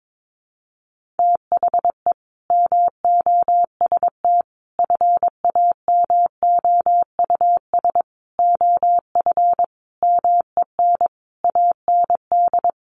Morse code, heard as a keyed tone, about 22 wpm, 700 Hz. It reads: THE MOST FAMOUS OF MEN AND
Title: Sentences from Top 1500 Words 22wpm